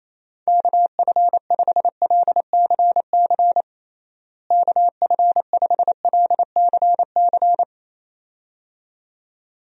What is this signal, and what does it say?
Morse code, heard as a keyed tone, about 28 wpm, 700 Hz. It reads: KF5LCC KF5LCC
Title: USA Call Signs - 6 Characters - Repeated 2x - 28wpm